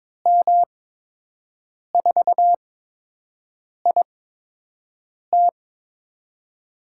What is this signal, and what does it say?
Morse code, heard as a keyed tone, about 22 wpm, spaced as for 6 wpm, 700 Hz. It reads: M4IT
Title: USA Call Signs - 5 Characters - 8x Character Spacing - 22wpm